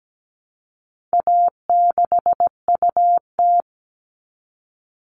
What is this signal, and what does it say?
Morse code, heard as a keyed tone, about 17 wpm, 700 Hz. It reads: A6UT
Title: All Cut Numbers - 4-Digit 17wpm